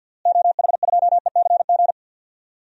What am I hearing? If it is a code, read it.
Morse code, heard as a keyed tone, about 52 wpm, 700 Hz. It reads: OH2ECG
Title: DXCC-100 Call Signs - 6 Characters - Repeated 3x - 50wpm